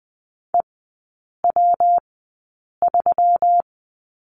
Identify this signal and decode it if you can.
Morse code, heard as a keyed tone, about 20 wpm, spaced as for 8 wpm, 700 Hz. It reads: EW3
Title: 121 Lesson 20e - Number 3 - Three Characters 20wpm